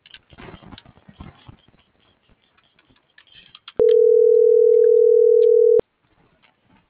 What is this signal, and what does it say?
Ringback tone (North America), 440 + 480 Hz.